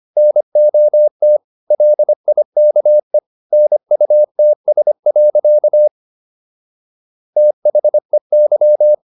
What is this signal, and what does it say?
Morse code, heard as a keyed tone, about 25 wpm, 600 Hz. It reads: NOT LIKE NUTS. THEY